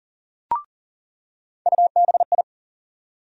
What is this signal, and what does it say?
Morse code, heard as a keyed tone, about 40 wpm, 700 Hz. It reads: UBI
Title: USA Call Sign Postfixes - 3 Letters 40wpm